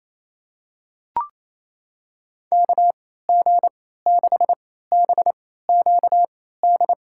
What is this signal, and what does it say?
Morse code, heard as a keyed tone, about 28 wpm, spaced as for 16 wpm, 700 Hz. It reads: KG6BQD
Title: USA Call Signs - 6 Characters - 3x Character Spacing - 28wpm